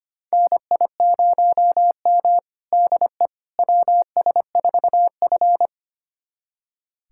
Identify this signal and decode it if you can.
Morse code, heard as a keyed tone, about 25 wpm, 700 Hz. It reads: NI0M DE WS4F
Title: USA Callsign DE Callsign - 4 Characters - 25wpm